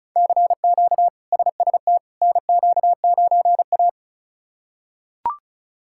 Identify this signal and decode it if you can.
Morse code, heard as a keyed tone, about 36 wpm, 700 Hz. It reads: CQ SST NQ9A
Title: K1USN SST QSO Practice - 35wpm